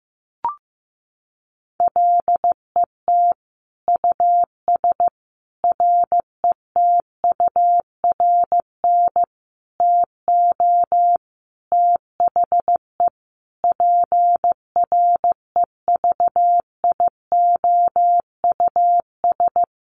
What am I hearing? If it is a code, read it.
Morse code, heard as a keyed tone, about 15 wpm, 700 Hz. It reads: LET US RETURN TO THE PREVIOUS
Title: Sentences from Top 1500 Words 15wpm